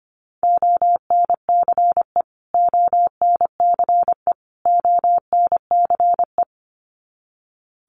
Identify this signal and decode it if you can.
Morse code, heard as a keyed tone, about 25 wpm, 700 Hz. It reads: ONCE ONCE ONCE
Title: Top 500 Words Repeated 25wpm